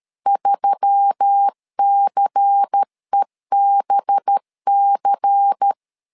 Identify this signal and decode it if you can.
Morse code, heard as a keyed tone, about 13 wpm, 800 Hz. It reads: 3CEBC
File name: LEZIONI / 5 LBVC / 5 LBVC 12wpm 1955